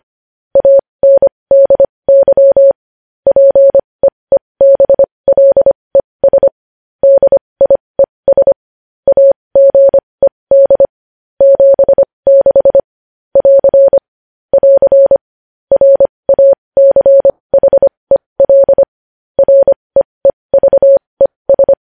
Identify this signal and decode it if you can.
Morse code, heard as a keyed tone, about 25 wpm, spaced as for 20 wpm, 550 Hz. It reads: ANDY PEEBLES DIES AGED 76 <AR> <AR> RACHEL REEVES
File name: podcast / News headlines-1x-25x20